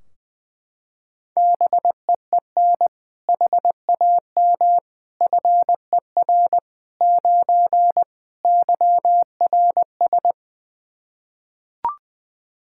Morse code, heard as a keyed tone, about 21 wpm, 700 Hz. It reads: BEEN HAM FER 9 YRS
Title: Practice QSO - Exchange 3 - More Info - 20wpm